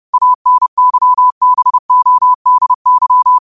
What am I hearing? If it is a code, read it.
Morse code, heard as a keyed tone, about 30 wpm, 1000 Hz. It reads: ANYBODY